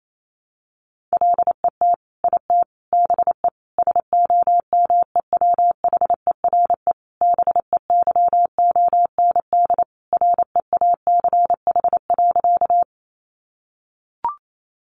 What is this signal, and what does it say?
Morse code, heard as a keyed tone, about 28 wpm, 700 Hz. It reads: LET IT BE SOMEWHERE BEYOND REACH.
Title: Sentences from Top 1400 Words 28wpm